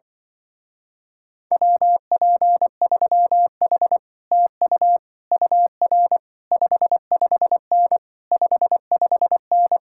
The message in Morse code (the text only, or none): WP3H TU UR 55N 55N